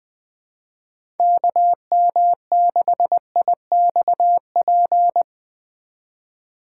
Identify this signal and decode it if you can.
Morse code, heard as a keyed tone, about 20 wpm, 700 Hz. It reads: KM6IXP
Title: USA Call Signs - 6x Character Spacing - 20wpm